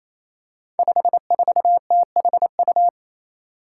Morse code, heard as a keyed tone, about 27 wpm, 700 Hz. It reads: 54THU
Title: Five Challenging Letter-Number Combinations 28wpm